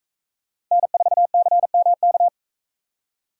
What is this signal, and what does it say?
Morse code, heard as a keyed tone, about 41 wpm, 700 Hz. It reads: N4CMK